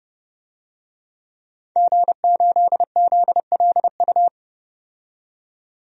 Morse code, heard as a keyed tone, about 30 wpm, 700 Hz. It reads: G8ZLU
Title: DXCC-100 Call Signs - 5 Characters - 30wpm